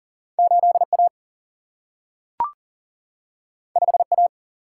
Morse code, heard as a keyed tone, about 40 wpm, 700 Hz. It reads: 8A 5A
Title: Single-Letter Single-Number 40wpm